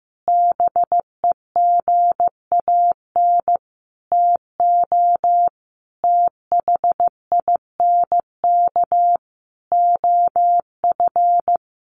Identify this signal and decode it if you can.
Morse code, heard as a keyed tone, about 15 wpm, 700 Hz. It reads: BEGAN TO THINK OF
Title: Sentences from Top 800 Words 15wpm